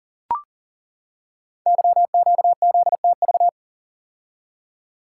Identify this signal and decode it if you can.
Morse code, heard as a keyed tone, about 42 wpm, 700 Hz. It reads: YQZTV
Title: Code Groups - 5 Letter:Numbers - 40wpm